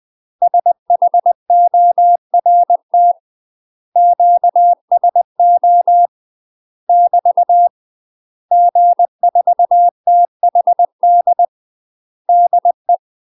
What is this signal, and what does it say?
Morse code, heard as a keyed tone, about 20 wpm, 700 Hz. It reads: SHORT QSO <BT> G4THD DE
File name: UK MASTER QSOs TRAINING / 20WPM / UK MASTER QSOs 0032